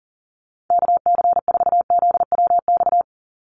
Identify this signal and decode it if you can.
Morse code, heard as a keyed tone, about 40 wpm, 700 Hz. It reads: KC4ZWX
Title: USA Call Signs 40wpm